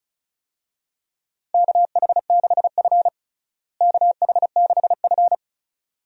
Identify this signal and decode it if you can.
Morse code, heard as a keyed tone, about 36 wpm, 700 Hz. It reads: KH6F KH6F
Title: USA Call Signs - 4 Characters - Repeated 2x - 35wpm